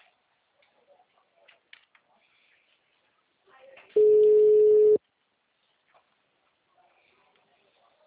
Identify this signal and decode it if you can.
Ringback tone (Europe, 425 Hz).